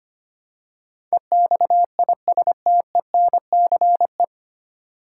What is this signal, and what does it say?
Morse code, heard as a keyed tone, about 25 wpm, 700 Hz. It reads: EXISTENCE
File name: practice / 25wpm / 9 Letter Words - First 3 Letters Given - 25wpm